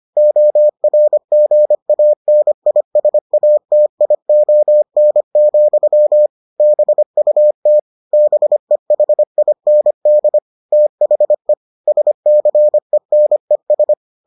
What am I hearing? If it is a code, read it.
Morse code, heard as a keyed tone, about 25 wpm, 600 Hz. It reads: ORGANISATION, BUT BEHIND THE SCENES